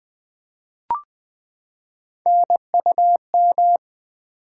Morse code, heard as a keyed tone, about 20 wpm, 700 Hz. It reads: NUM
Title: Build A Word - 3 Letters At A Time - 6-Letter Words - 20wpm